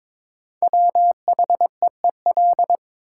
Morse code, heard as a keyed tone, about 22 wpm, 700 Hz. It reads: WHEEL
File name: 5 Letter Words - First 3 Letters Given - 22wpm